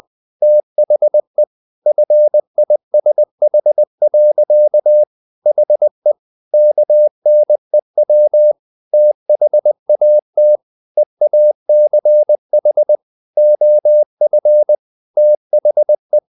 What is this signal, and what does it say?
Morse code, heard as a keyed tone, about 20 wpm, 600 Hz. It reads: THE FISH. HE KNEW THAT EACH OF THE
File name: the old man and the sea part 2 20wpm